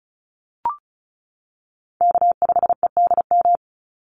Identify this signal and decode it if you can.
Morse code, heard as a keyed tone, about 34 wpm, 700 Hz. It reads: K5EDM
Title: Call Signs - 35wpm